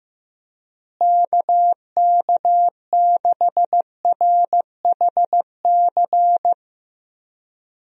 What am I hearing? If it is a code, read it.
Morse code, heard as a keyed tone, about 16 wpm, 700 Hz. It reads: KK6RHC
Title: USA Call Signs - 6 Characters - Repeated 3x - 15wpm